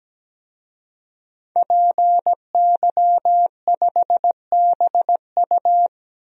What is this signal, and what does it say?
Morse code, heard as a keyed tone, about 17 wpm, 700 Hz. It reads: PY5BU
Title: DXCC-100 Call Signs - 5 Characters - 17wpm